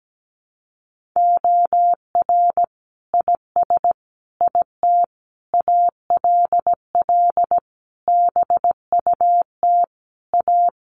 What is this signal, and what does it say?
Morse code, heard as a keyed tone, about 17 wpm, 700 Hz. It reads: OR IS IT ALL BUT A
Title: Sentences from Top 1500 Words 17wpm